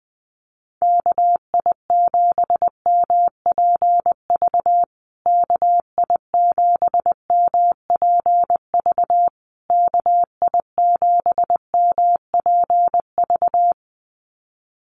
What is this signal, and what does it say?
Morse code, heard as a keyed tone, about 20 wpm, 700 Hz. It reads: KI7MPV KI7MPV KI7MPV
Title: USA Call Signs Repeated 3x 20wpm